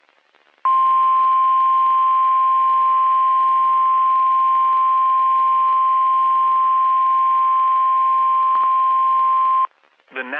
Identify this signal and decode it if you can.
Emergency Alert System (SAME) tones (the 1050 Hz attention tone). It is heard from 1.1 to 9.6 s.
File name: eas messages / EAS WXM31 Elmira